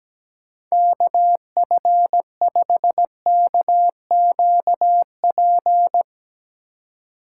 Morse code, heard as a keyed tone, about 17 wpm, 700 Hz. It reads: KF5KQP
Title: USA Call Signs - 6 Characters - 2x Character Spacing - 17wpm